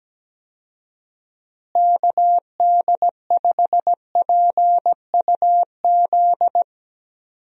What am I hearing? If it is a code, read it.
Morse code, heard as a keyed tone, about 17 wpm, 700 Hz. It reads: KD5PUZ
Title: USA Call Signs - 6 Characters - Repeated 2x - 17wpm